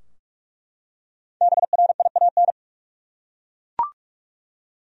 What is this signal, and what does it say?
Morse code, heard as a keyed tone, about 47 wpm, 700 Hz. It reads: BRIAN
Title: USA Common Names - Max 5 Letters - 45wpm